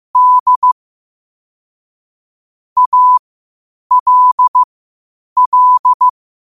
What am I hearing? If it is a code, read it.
Morse code, heard as a keyed tone, about 15 wpm, spaced as for 8 wpm, 1000 Hz. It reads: D ALL
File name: qotdfiles / 8wpm